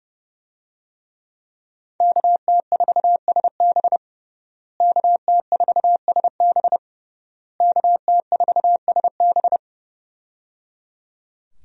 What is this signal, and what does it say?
Morse code, heard as a keyed tone, about 29 wpm, 700 Hz. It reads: KT4SB KT4SB KT4SB
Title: USA Call Signs - 5 Characters - Repeated 3x - 30wpm